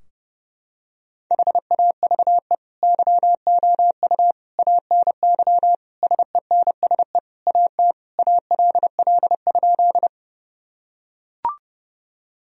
Morse code, heard as a keyed tone, about 30 wpm, 700 Hz. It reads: HAVE YOU ANY SENSE AT ALL?
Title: Sentences from Top 1000 Words - Review Part 4 - 30wpm